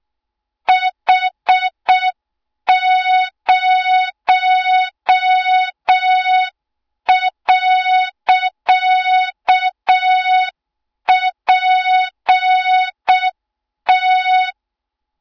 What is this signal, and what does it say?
Morse code, heard as a keyed tone, about 6 wpm, 3700 Hz. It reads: H0.PT
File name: Morze-HA2MN / hang / 20 GYAKORLAT